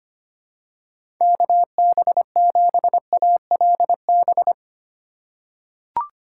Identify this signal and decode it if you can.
Morse code, heard as a keyed tone, about 25 wpm, 700 Hz. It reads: KB7ALB
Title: USA Call Signs - 6 Characters - 3x Character Spacing - 25wpm